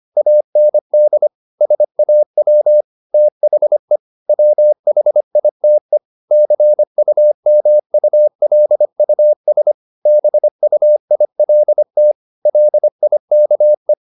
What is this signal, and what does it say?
Morse code, heard as a keyed tone, about 25 wpm, 600 Hz. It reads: AND SAW THE WHITE CUMULUS BUILT LIKE
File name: the old man and the sea part 1 25wpm